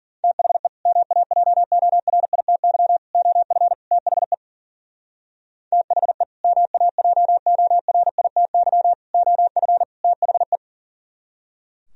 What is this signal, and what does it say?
Morse code, heard as a keyed tone, about 52 wpm, 700 Hz. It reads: THE MAJORITY OF THE THE MAJORITY OF THE